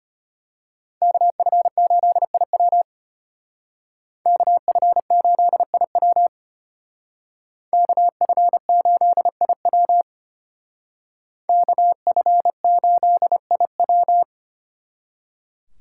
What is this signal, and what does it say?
Morse code, heard as a keyed tone, about 30 wpm, 700 Hz. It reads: KF8IW KF8IW KF8IW KF8IW
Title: Speed Racing - USA 2x2 Call Signs 25wpm